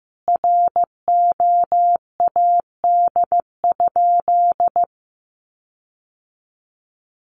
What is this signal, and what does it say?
Morse code, heard as a keyed tone, about 15 wpm, 700 Hz. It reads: ROAD?
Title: Sentences from Top 1400 Words 15wpm